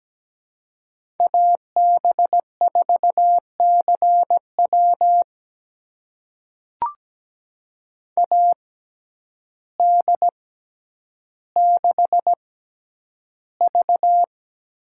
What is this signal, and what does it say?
Morse code, heard as a keyed tone, about 17 wpm, spaced as for 9 wpm, 700 Hz. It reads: AB4CW AD6V
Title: USA Call Signs - 5 Characters - 6x Character Spacing - 17wpm